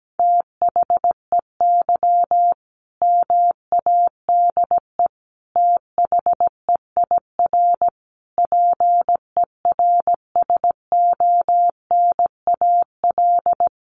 Morse code, heard as a keyed tone, about 17 wpm, 700 Hz. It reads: THEY MADE THEIR PERSONAL
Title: Sentences from Top 1100 Words 17wpm